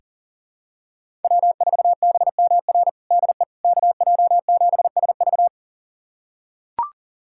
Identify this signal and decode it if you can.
Morse code, heard as a keyed tone, about 40 wpm, 700 Hz. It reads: W4BMR DE KJ7SV
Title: USA Callsign DE Callsign - 5 Characters - 40wpm